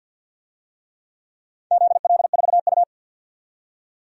Morse code, heard as a keyed tone, about 49 wpm, 700 Hz. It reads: ZL4U
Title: DXCC-100 Call Signs - 4 Characters - 50wpm